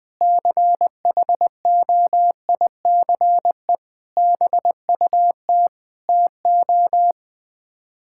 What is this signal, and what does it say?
Morse code, heard as a keyed tone, about 20 wpm, 700 Hz. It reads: CHOICE BUT TO
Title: Build A Phrase - 5-Word Phrases - 20wpm